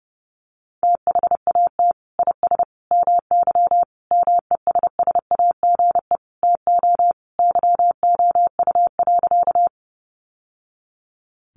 Morse code, heard as a keyed tone, about 30 wpm, 700 Hz. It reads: THAT IS MY MESSAGE TO YOU.
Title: Sentences from Top 1100 Words 30wpm